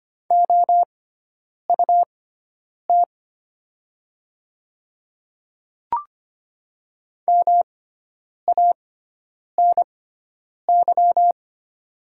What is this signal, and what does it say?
Morse code, heard as a keyed tone, about 25 wpm, spaced as for 9 wpm, 700 Hz. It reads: OUT MANY
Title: 6x Character Spacing - Top 100 Words - 25wpm